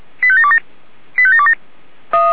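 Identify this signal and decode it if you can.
Five-tone selective call (ZVEI), heard as quick runs of five tones, tones of about 70 ms. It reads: 85727 85727
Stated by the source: ZVEI five-tone call received by SDR.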